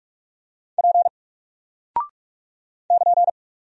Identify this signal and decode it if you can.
Morse code, heard as a keyed tone, about 46 wpm, 700 Hz. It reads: P <KN>
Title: Single Character (Letter:Number:Punctuation:Prosign) 45wpm